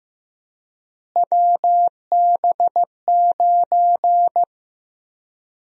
Morse code, heard as a keyed tone, about 15 wpm, 700 Hz. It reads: WB9
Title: USA Call Sign Prefixes - Repeated 2x - 15wpm